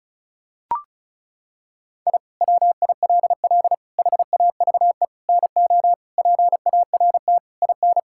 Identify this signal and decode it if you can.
Morse code, heard as a keyed tone, about 34 wpm, 700 Hz. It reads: I WILL HAVE NO PART IN
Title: Sentences from Top 1000 Words - Review Part 5 - 35wpm